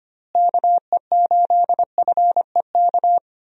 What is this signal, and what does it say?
Morse code, heard as a keyed tone, about 25 wpm, 700 Hz. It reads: KE8FEK
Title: USA Call Signs - 6 Characters - Repeated 3x - 25wpm